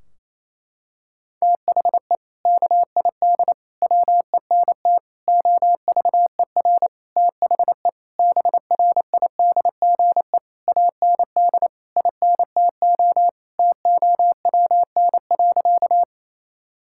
Morse code, heard as a keyed tone, about 28 wpm, 700 Hz. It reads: THE KID WENT OVER THE BRIDGE AND INTO TOWN.
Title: Sentences from Top 1600 Words 28wpm